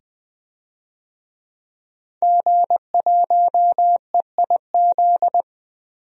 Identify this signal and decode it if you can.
Morse code, heard as a keyed tone, about 21 wpm, 700 Hz. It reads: G1EIZ
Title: DXCC-100 Call Signs - 5 Characters - 20wpm